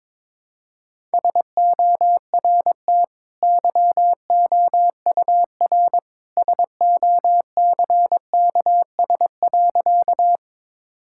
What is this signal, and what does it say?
Morse code, heard as a keyed tone, about 22 wpm, 690 Hz. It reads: SORT YOUR SOCKS.
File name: practice / 22wpm / Easy Sentences with Hints - 22wpm